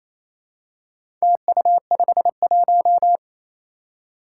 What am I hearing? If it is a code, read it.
Morse code, heard as a keyed tone, about 28 wpm, 700 Hz. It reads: TU51
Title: Four Challenging Letter-Number Combinations 28wpm